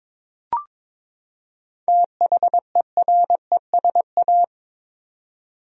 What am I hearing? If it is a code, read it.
Morse code, heard as a keyed tone, about 22 wpm, 700 Hz. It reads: THERESA
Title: USA Common Names - 22wpm